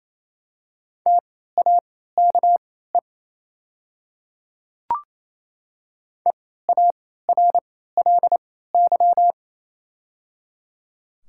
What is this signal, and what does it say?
Morse code, heard as a keyed tone, about 28 wpm, spaced as for 16 wpm, 690 Hz. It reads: TAKE EARLY
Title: 3x Character Spacing - Top 500 Words - 28wpm